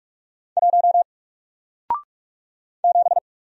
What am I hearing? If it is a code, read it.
Morse code, heard as a keyed tone, about 44 wpm, 700 Hz. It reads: 1 7
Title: Single Character (Letter:Number:Punctuation:Prosign) 45wpm